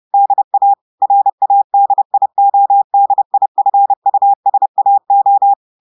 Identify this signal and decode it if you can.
Morse code, heard as a keyed tone, about 30 wpm, 800 Hz. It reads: DA RADIODIFUSAO
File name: audios / portugais / portugais texte en clair 1 vitesse 30 mots minutes